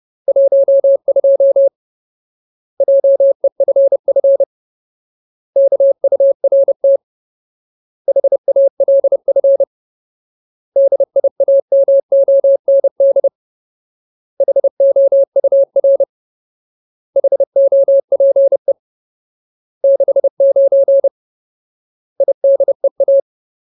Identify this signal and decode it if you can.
Morse code, heard as a keyed tone, about 30 wpm, 550 Hz. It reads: 12 JEFF KURT HALF DIAMOND HOUR HOPE 69 IDEA
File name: Ultimate Word List / 30WPM Ultimate